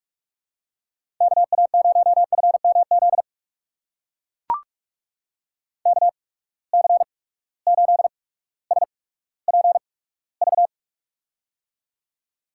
Morse code, heard as a keyed tone, about 42 wpm, spaced as for 19 wpm, 700 Hz. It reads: KA0FMZ KC8SPV
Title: USA Call Signs - 6 Characters - 8x Character Spacing - 45wpm